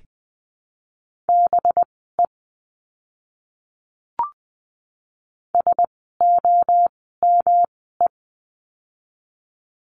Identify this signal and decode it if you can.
Morse code, heard as a keyed tone, about 20 wpm, spaced as for 14 wpm, 700 Hz. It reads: BE SOME
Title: 237 Lesson 38f - Letter q - Words 20wpm